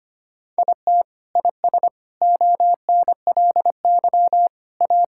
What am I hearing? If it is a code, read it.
Morse code, heard as a keyed tone, about 25 wpm, 700 Hz. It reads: IT IS ONLY A
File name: Sets of 9 Words - 25wpm